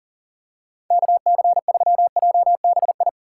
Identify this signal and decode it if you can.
Morse code, heard as a keyed tone, about 40 wpm, 700 Hz. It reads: KC3JBI
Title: USA Call Signs - 3x Character Spacing - 40wpm